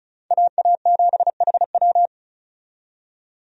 Morse code, heard as a keyed tone, about 36 wpm, 700 Hz. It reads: AA7HW
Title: USA Call Signs - 5 Characters - 35wpm